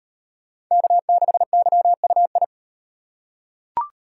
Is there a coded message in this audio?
Yes: Morse code from 0.7 to 2.4 s, heard as a keyed tone, about 37 wpm, 700 Hz. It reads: K6YUI